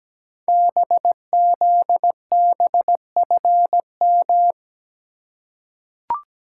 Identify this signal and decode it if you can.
Morse code, heard as a keyed tone, about 17 wpm, 700 Hz. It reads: BZBFM
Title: Code Groups - 5 Letters - 17wpm